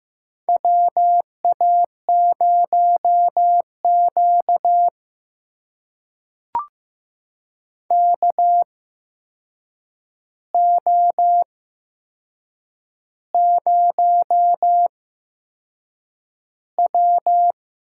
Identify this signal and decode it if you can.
Morse code, heard as a keyed tone, about 15 wpm, spaced as for 6 wpm, 700 Hz. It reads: WA0Q KO0W
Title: USA Call Signs - 4 Characters - 8x Character Spacing - 15wpm